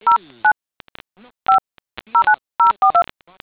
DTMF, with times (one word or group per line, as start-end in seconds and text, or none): 0.1-0.5 09
1.5-1.6 5
2.1-3.0 09042